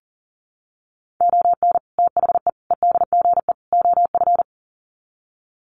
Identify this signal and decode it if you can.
Morse code, heard as a keyed tone, about 40 wpm, 700 Hz. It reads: ON THE EDGE OF
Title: Sets of 4 Words 40wpm